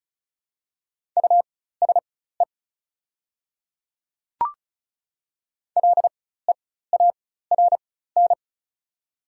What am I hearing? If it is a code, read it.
Morse code, heard as a keyed tone, about 36 wpm, spaced as for 16 wpm, 690 Hz. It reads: USE LEARN
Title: 4x Character Spacing - Top 300 Words - 35wpm